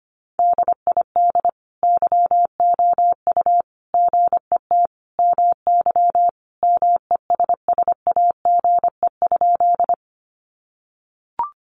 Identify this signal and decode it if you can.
Morse code, heard as a keyed tone, about 25 wpm, 700 Hz. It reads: DID YOU GET MY MESSAGE?
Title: Sentences from Top 1100 Words 25wpm